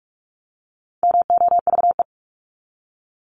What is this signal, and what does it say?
Morse code, heard as a keyed tone, about 44 wpm, 700 Hz. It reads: MOVE
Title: Top 500 Words 45wpm